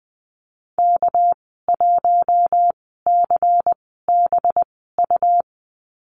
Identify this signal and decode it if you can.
Morse code, heard as a keyed tone, about 20 wpm, spaced as for 15 wpm, 700 Hz. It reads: K1CBU
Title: 207 Lesson 33g - Letter j - Call Signs 20wpm